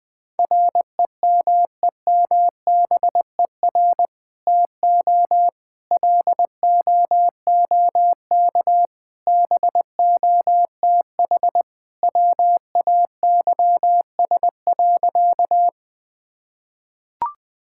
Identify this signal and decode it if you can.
Morse code, heard as a keyed tone, about 20 wpm, 700 Hz. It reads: REMEMBER TO LOOK BOTH WAYS.
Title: Sentences with 2-Word Phrases - Spoken Beginning - 20wpm